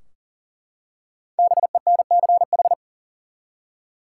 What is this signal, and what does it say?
Morse code, heard as a keyed tone, about 40 wpm, 700 Hz. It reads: BENCH
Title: Five Letter Words - 40wpm